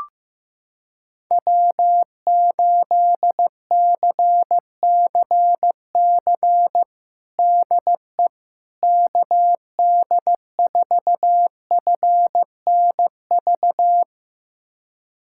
Morse code, heard as a keyed tone, about 15 wpm, 700 Hz. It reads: W8CCC DE KD4FNV
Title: USA Callsign DE Callsign 15wpm